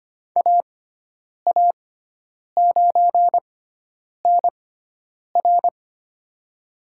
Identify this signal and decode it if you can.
Morse code, heard as a keyed tone, about 25 wpm, spaced as for 9 wpm, 700 Hz. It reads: AA9NR
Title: USA Call Signs - 5 Characters - 6x Character Spacing - 25wpm